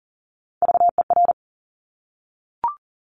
Morse code, heard as a keyed tone, about 41 wpm, 690 Hz. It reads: VER
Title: Top Three Letter Combinations in Words 40wpm